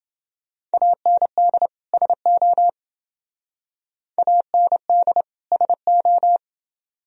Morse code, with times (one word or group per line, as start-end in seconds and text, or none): 0.7-1.7 AND
1.9-2.7 SO
4.2-5.2 AND
5.5-6.4 SO